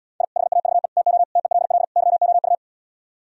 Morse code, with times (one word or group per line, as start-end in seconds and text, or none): 0.2-0.2 I
0.4-0.9 HAVE
1.0-1.2 TO
1.4-1.8 TELL
2.0-2.5 YOU